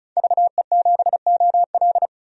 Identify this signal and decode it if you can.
Morse code, heard as a keyed tone, about 36 wpm, 700 Hz. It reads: VE7OL
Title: DXCC-100 Call Signs - 5 Characters - 35wpm